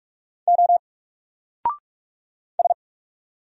Morse code, heard as a keyed tone, about 46 wpm, 700 Hz. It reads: O S
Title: Single Letter-Number 45wpm